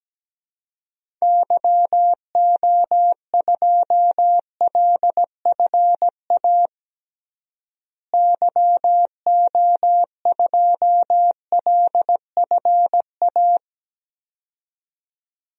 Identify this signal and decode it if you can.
Morse code, heard as a keyed tone, about 17 wpm, 700 Hz. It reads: YO2LFA YO2LFA
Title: DXCC-100 Call Signs - 6 Characters - Repeated 2x - 17wpm